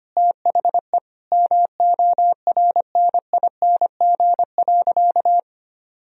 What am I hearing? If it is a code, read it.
Morse code, heard as a keyed tone, about 25 wpm, 700 Hz. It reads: THE MORNING.
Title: Sentences from Top 1000 Words - Review Part 1 - 25wpm